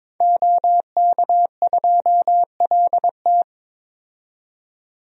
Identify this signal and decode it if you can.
Morse code, heard as a keyed tone, about 21 wpm, 700 Hz. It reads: OK2LT
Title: DXCC-100 Call Signs - 5 Characters - 22wpm